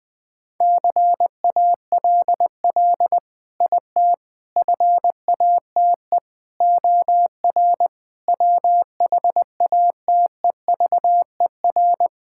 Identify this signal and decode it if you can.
Morse code, heard as a keyed tone, about 20 wpm, 700 Hz. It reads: CALL IT FATE OR WHATEVER
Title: Sentences with 7-Word Phrases - 20wpm